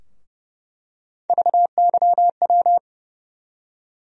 Morse code, heard as a keyed tone, about 31 wpm, 700 Hz. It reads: VYW
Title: USA Call Sign Postfixes - 3 Letters 30wpm